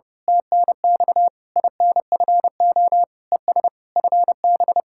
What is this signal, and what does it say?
Morse code, heard as a keyed tone, about 29 wpm, 700 Hz. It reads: TNX INFO ES FB
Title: Speed-Racing Common QSO Phrases - 30wpm